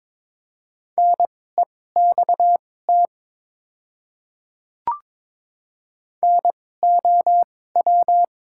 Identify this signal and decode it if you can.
Morse code, heard as a keyed tone, about 23 wpm, spaced as for 16 wpm, 700 Hz. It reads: NEXT NOW
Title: 2x Character Spacing - Top 500 Words - 22wpm